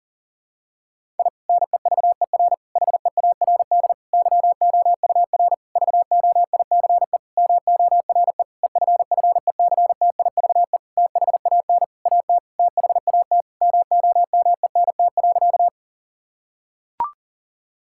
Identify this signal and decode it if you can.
Morse code, heard as a keyed tone, about 37 wpm, 700 Hz. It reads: I NEVER HEARD YOUR VOICE MORE EFFECTIVE THAN AT THAT MOMENT.
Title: Sentences from Top 1000 Words - Review Part 6 - 40wpm